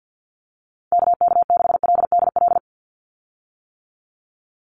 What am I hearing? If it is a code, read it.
Morse code, heard as a keyed tone, about 50 wpm, 700 Hz. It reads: KK6LDL
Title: USA Call Signs 50wpm